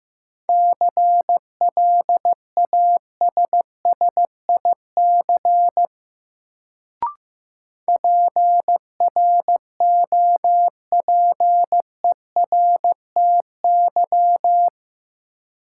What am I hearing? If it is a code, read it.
Morse code, heard as a keyed tone, about 15 wpm, 700 Hz. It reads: CLASSIC PROPERTY
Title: Sentences from Top 1000 Words - New Word Drill - 15wpm